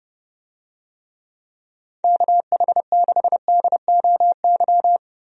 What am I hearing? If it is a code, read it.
Morse code, heard as a keyed tone, about 30 wpm, 700 Hz. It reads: KH6DOY
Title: DXCC-100 Call Signs - 6 Characters - Repeated 2x - 30wpm